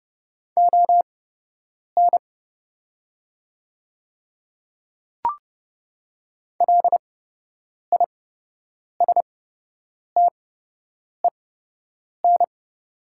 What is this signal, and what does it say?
Morse code, heard as a keyed tone, about 30 wpm, spaced as for 8 wpm, 700 Hz. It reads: ON LISTEN
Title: 8x Character Spacing - Top 500 Words - 30wpm